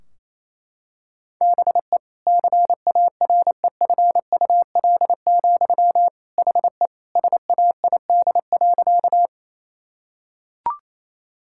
Morse code, heard as a keyed tone, about 28 wpm, 700 Hz. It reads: BE CAREFUL, HE SAID.
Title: Sentences from Top 1600 Words 28wpm